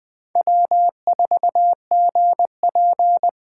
Morse code, heard as a keyed tone, about 25 wpm, 700 Hz. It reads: W4GP
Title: Speed Racing - USA 1x2 Call Signs 20wpm